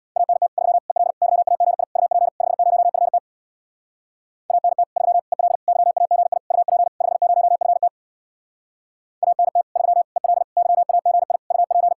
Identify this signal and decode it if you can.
Morse code, heard as a keyed tone, about 66 wpm, 700 Hz. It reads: ANT 3 EL YAGI UP 50FT ANT 3 EL YAGI UP 50FT ANT 3 EL YAGI UP